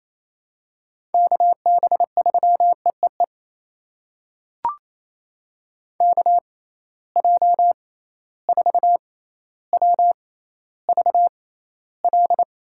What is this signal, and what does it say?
Morse code, heard as a keyed tone, about 28 wpm, spaced as for 14 wpm, 700 Hz. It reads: KB3EEE KJ4WVL
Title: USA Call Signs - 6x Character Spacing - 28wpm